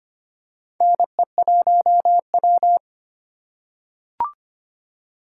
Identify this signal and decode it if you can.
Morse code, heard as a keyed tone, about 25 wpm, 700 Hz. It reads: NE1W
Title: USA Call Signs - 4 Characters - 6x Character Spacing - 25wpm